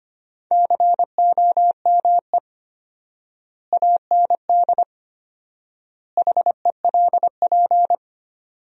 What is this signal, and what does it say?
Morse code, heard as a keyed tone, about 25 wpm, 700 Hz. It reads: COME AND HELP
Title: 4x Word-Spacing - Sentences from Top 200 Words - 25wpm